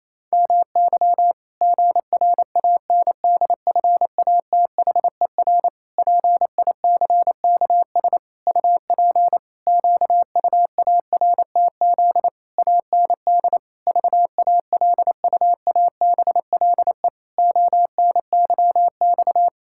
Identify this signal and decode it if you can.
Morse code, heard as a keyed tone, about 28 wpm, 700 Hz. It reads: MY GRANDFATHER PICKS UP QUARTZ AND VALUABLE ONYX
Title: Pangram Challenge v2 - 28wpm